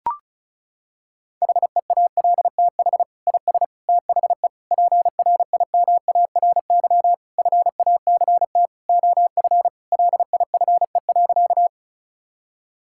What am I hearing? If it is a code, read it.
Morse code, heard as a keyed tone, about 36 wpm, 700 Hz. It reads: HEALTH IS THE PRIMARY FACT OF LIFE.
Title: Sentences from Top 1000 Words - Review Part 5 - 35wpm